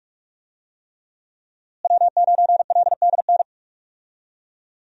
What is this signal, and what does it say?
Morse code, heard as a keyed tone, about 46 wpm, 700 Hz. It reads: W9LDN
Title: USA Call Signs - 5 Characters - Repeated 3x - 45wpm